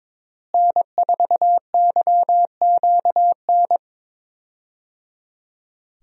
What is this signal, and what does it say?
Morse code, heard as a keyed tone, about 21 wpm, 700 Hz. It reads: N4YQN
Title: USA Call Signs - 5 Characters - Repeated 3x - 22wpm